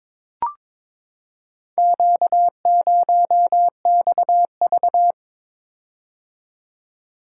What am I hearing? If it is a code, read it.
Morse code, heard as a keyed tone, about 22 wpm, 700 Hz. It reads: Q0XV
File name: practice / 22wpm / Four Letter-Number Combinations - 22wpm